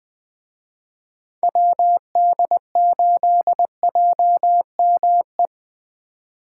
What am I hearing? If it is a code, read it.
Morse code, heard as a keyed tone, about 20 wpm, 700 Hz. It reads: WD8JME
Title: Speed Racing - USA 2x3 Call Signs 17wpm